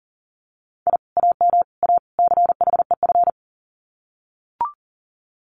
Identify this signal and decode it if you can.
Morse code, heard as a keyed tone, about 40 wpm, 700 Hz. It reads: I AM A CHEF
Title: Common QSO Phrases Encore 40wpm